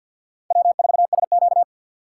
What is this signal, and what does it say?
Morse code, heard as a keyed tone, about 49 wpm, 700 Hz. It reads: W4SQ
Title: USA Call Signs - 4 Characters - 6x Character Spacing - 50wpm